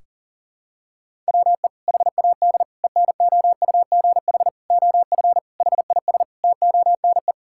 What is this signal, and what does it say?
Morse code, heard as a keyed tone, about 40 wpm, 700 Hz. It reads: WE HAD ENOUGH OF HIS TONE
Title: Sentences from Top 1600 Words 40wpm